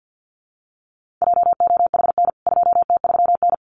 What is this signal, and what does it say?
Morse code, heard as a keyed tone, about 48 wpm, 700 Hz. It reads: JOHN 2T3N
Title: CWT Exchange 50wpm